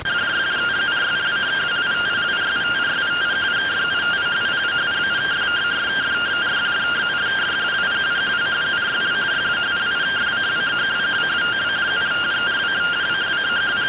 Two-tone teleprinter signal, about 170 Hz shift.